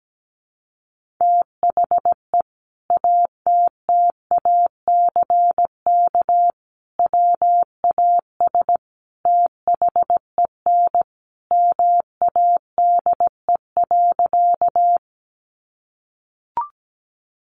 Morse code, heard as a keyed tone, about 17 wpm, 700 Hz. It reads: THE ATTACK WAS THEN MADE.
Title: Sentences from Top 1500 Words 17wpm